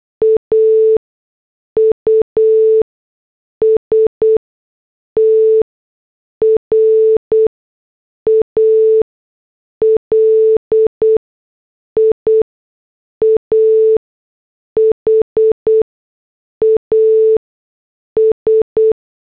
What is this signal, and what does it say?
Morse code, heard as a keyed tone, about 8 wpm, spaced as for 6 wpm, 440 Hz. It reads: AUSTRALIA HAS
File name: morse / headlines